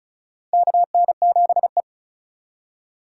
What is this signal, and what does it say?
Morse code, heard as a keyed tone, about 34 wpm, 700 Hz. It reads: KN7E
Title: USA Call Signs - 4 Characters - 4x Character Spacing - 35wpm